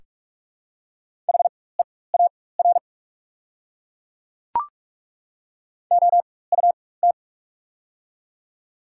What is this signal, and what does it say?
Morse code, heard as a keyed tone, about 43 wpm, spaced as for 21 wpm, 700 Hz. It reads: HEAR OUT
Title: 4x Character Spacing - Top 300 Words - 45wpm